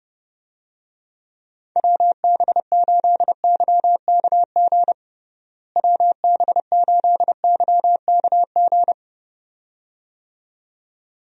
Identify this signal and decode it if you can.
Morse code, heard as a keyed tone, about 30 wpm, 700 Hz. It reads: WB8YKG WB8YKG
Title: USA Call Signs - 6 Characters - Repeated 2x - 30wpm